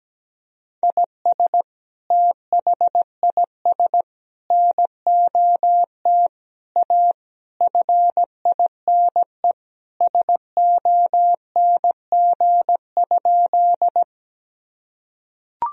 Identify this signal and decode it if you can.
Morse code, heard as a keyed tone, about 17 wpm, 700 Hz. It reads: IS THIS NOT A FINE SONG?
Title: Sentences from Top 1000 Words - Review Part 2 - 17wpm